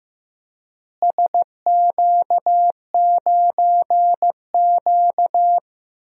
Morse code, heard as a keyed tone, about 15 wpm, 700 Hz. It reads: SQ9Q
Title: DXCC-100 Call Signs - 4 Characters - Repeated 3x - 15wpm